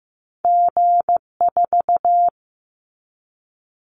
Morse code, heard as a keyed tone, about 15 wpm, 700 Hz. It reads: G4
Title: Callsign Prefixes for Top 25 DXCC - Repeated 3x - 15wpm